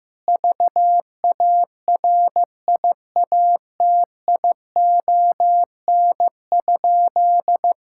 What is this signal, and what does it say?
Morse code, heard as a keyed tone, about 15 wpm, 700 Hz. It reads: VARIATION?
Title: Sentences from Top 1600 Words 15wpm